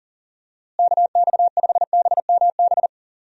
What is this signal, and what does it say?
Morse code, heard as a keyed tone, about 40 wpm, 700 Hz. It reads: KX5BMB
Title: USA Call Signs - 6 Characters - Repeated 2x - 40wpm